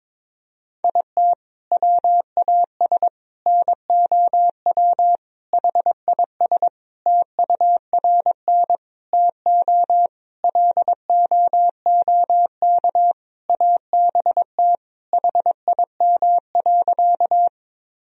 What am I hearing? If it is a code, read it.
Morse code, heard as a keyed tone, about 21 wpm, 700 Hz. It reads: IT WAS NOW HIS TURN TO LOOK ABT HIM.
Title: Sentences from Top 200 Words and Abbreviations 22wpm